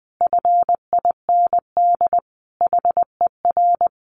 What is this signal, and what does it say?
Morse code, heard as a keyed tone, about 20 wpm, 700 Hz. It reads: FIND HER
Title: Easy Phrases 20wpm